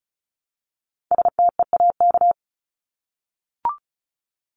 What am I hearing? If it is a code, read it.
Morse code, heard as a keyed tone, about 34 wpm, 700 Hz. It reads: STEAK
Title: Top 1251-1500 Words 35wpm